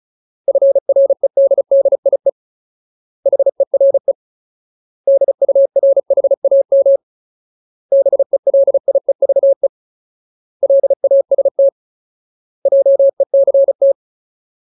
Morse code, heard as a keyed tone, about 34 wpm, 550 Hz. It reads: FREDDIE HERE DURHAM BELIEVE LAST JECT
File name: Ultimate Word List / 35WPM Ultimate